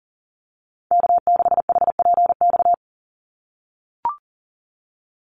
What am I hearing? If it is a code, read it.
Morse code, heard as a keyed tone, about 40 wpm, 700 Hz. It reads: K6HPX
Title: NA SOTA Activator Callsigns 40wpm